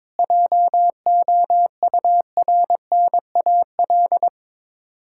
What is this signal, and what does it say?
Morse code, heard as a keyed tone, about 22 wpm, 690 Hz. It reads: JOURNAL
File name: Difficult Words with Hints - 22wpm